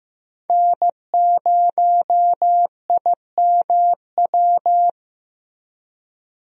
Morse code, heard as a keyed tone, about 15 wpm, 700 Hz. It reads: N0IMW
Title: USA Call Signs - 5 Characters - Repeated 2x - 15wpm